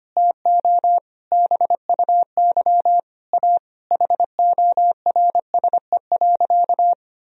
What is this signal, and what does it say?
Morse code, heard as a keyed tone, about 25 wpm, 700 Hz. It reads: TO BUY A HORSE.
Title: Sentences from Top 1000 Words - Review Part 4 - 25wpm